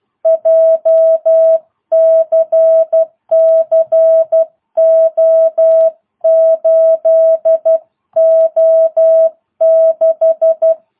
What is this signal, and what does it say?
Morse code, heard as a keyed tone, about 12 wpm, 660 Hz. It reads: JCC O8O6